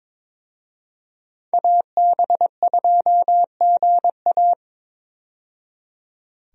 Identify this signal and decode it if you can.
Morse code, heard as a keyed tone, about 22 wpm, 700 Hz. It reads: AB2GA
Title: USA Call Signs - 5 Characters - 22wpm